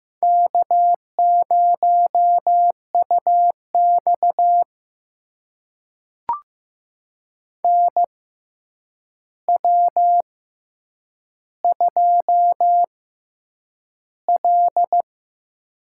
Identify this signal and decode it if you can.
Morse code, heard as a keyed tone, about 15 wpm, spaced as for 8 wpm, 700 Hz. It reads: K0UX NW2L
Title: USA Call Signs - 4 Characters - 6x Character Spacing - 15wpm